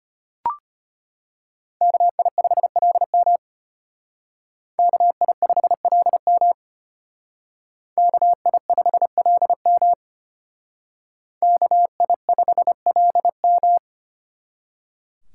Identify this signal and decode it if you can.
Morse code, heard as a keyed tone, about 32 wpm, 700 Hz. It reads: KI5LM KI5LM KI5LM KI5LM